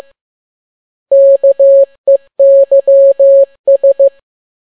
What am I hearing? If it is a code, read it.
Morse code, heard as a keyed tone, about 15 wpm, 550 Hz. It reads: KEYS
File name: n6tt p1 15C13